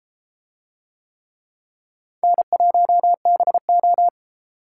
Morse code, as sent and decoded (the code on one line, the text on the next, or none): -. .---- -... ---
N1BO